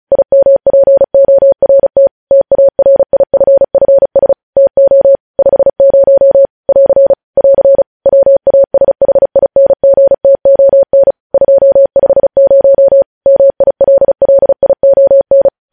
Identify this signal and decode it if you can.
Morse code, heard as a keyed tone, about 36 wpm, 550 Hz. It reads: IMPORT TARIFFS TO 50 <AR> <AR> WASHINGTON 250 MILLION
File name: podcast / News headlines-1x-35x35